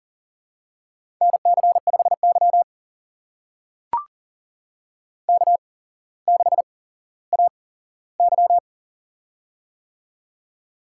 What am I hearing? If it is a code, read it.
Morse code, heard as a keyed tone, about 42 wpm, spaced as for 17 wpm, 700 Hz. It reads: NC5Y K6AY
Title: USA Call Signs - 4 Characters - 8x Character Spacing - 40wpm